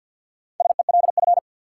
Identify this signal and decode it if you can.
Morse code, heard as a keyed tone, about 53 wpm, 700 Hz. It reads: SELF
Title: Sentences from Top 1000 Words - Review Part 1 - New Word Drill - 50wpm